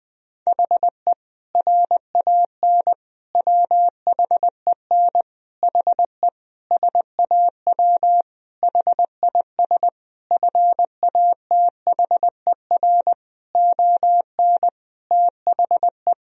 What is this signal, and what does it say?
Morse code, heard as a keyed tone, about 20 wpm, 700 Hz. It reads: HE RAN WHEN HE SAW HIS FATHER ON THE
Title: Sentences from Top 1000 Words - Review Part 3 - 20wpm